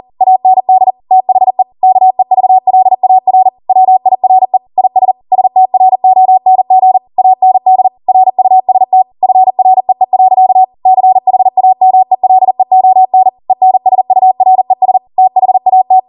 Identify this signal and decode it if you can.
Morse code, heard as a keyed tone, about 40 wpm, 750 Hz. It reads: AND THE KEVLAR WIRE IS STRONG AND RUST FREE. CHAMELEON ENSURES THAT